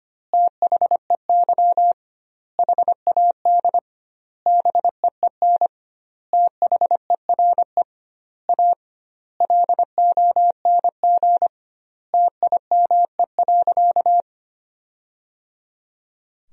Morse code, heard as a keyed tone, about 25 wpm, 700 Hz. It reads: THEY HAD BEEN THERE A LONG TIME.
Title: 2x Word-Spacing - Sentences from Top 400 Words - 25wpm